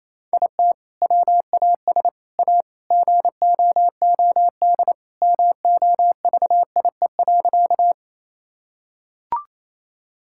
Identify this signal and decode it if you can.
Morse code, heard as a keyed tone, about 28 wpm, 700 Hz. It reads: IT WAS A GOOD MOVIE.
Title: Sentences from Top 1000 Words - Review Part 5 - 28wpm